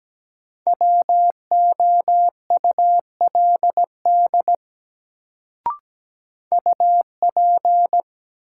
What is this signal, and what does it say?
Morse code, heard as a keyed tone, about 17 wpm, 700 Hz. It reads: WOULD UP
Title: Sentences from Top 100 Words - New Word Drill - 17wpm